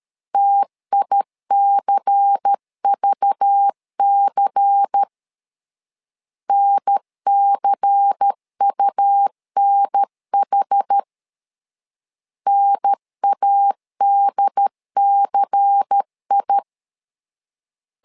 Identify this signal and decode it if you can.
Morse code, heard as a keyed tone, about 13 wpm, 800 Hz. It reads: TICVC NCUNH NADCI
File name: LEZIONI / 5 LBVC / 5 LBVC 12wpm 1955